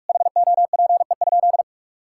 Morse code, heard as a keyed tone, about 41 wpm, 700 Hz. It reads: HOPE?
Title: Sentences from Top 1000 Words - Review Part 5 - 45wpm